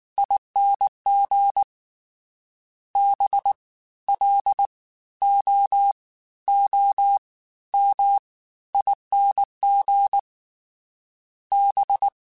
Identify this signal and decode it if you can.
Morse code, heard as a keyed tone, about 19 wpm, 800 Hz. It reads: ING B L O O M ING B
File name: Special INGs-mp3s / Special INGs-mp3s / Fast ing @18